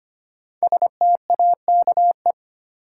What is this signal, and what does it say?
Morse code, heard as a keyed tone, about 25 wpm, 700 Hz. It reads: STAKE
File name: 5 Letter Words - First 3 Letters Given - 25wpm